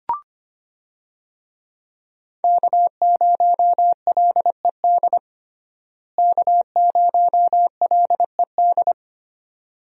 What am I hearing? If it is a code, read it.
Morse code, heard as a keyed tone, about 25 wpm, 690 Hz. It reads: K0LED K0LED
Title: USA Call Signs - 5 Characters - Repeated 2x - 25wpm